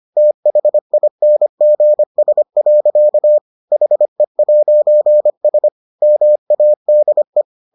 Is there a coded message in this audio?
Yes: Morse code from 0.2 to 7.4 s, heard as a keyed tone, about 25 wpm, 600 Hz. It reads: THINGS. HE'S MADE